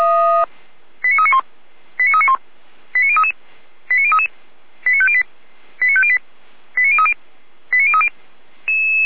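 Five-tone selective call (ZVEI), heard as quick runs of five tones, tones of about 80 ms. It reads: 89391 89391 89033 89033 89508 89508 89030 89030
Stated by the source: Received by SDR, ZVEI five-tone selective call 89145.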